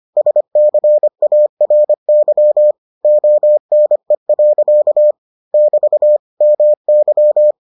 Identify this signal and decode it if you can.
Morse code, heard as a keyed tone, about 25 wpm, 600 Hz. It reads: SCARY ONE. <BT> MY